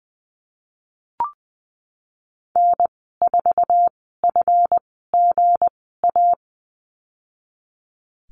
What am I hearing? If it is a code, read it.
Morse code, heard as a keyed tone, about 20 wpm, spaced as for 15 wpm, 700 Hz. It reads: N4FGA
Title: 167 Lesson 27g - Number 7 - Call Signs 20wpm